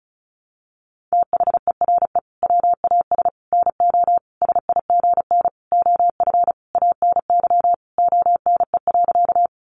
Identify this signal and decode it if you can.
Morse code, heard as a keyed tone, about 34 wpm, 700 Hz. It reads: THERE WAS NO SIGN OF ANY ONE.
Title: Sentences from Top 1000 Words 35wpm